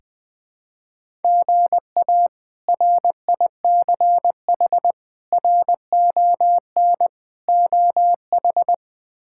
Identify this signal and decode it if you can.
Morse code, heard as a keyed tone, about 21 wpm, 700 Hz. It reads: GA RICH RON OH
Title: K1USN SST QSO Practice - 20wpm